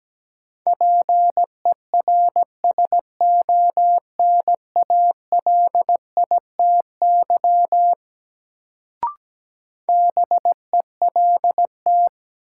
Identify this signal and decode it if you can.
Morse code, heard as a keyed tone, about 17 wpm, 700 Hz. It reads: PERSONALITY BELT
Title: Sentences from Top 1400 Words - New Word Drill Alt - 17wpm